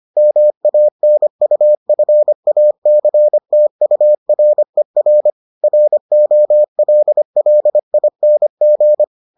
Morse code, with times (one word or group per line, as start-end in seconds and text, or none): 0.2-5.3 MANUFACTURER
5.6-9.0 ROLLING